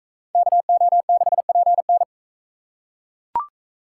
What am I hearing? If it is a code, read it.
Morse code, heard as a keyed tone, about 38 wpm, 700 Hz. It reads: KO6PN